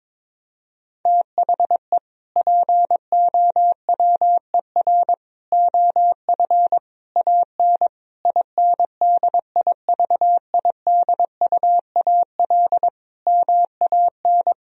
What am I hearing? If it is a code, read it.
Morse code, heard as a keyed tone, about 22 wpm, 700 Hz. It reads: THE POWER OF AN INDIVIDUAL MAN
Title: Sentences from Top 1000 Words - Review Part 4 - 22wpm